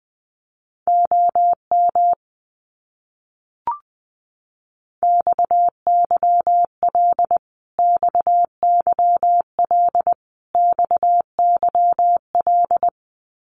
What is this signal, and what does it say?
Morse code, heard as a keyed tone, about 20 wpm, 700 Hz. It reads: OM XYL XYL XYL
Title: Abbreviations Repeated 3x - 20wpm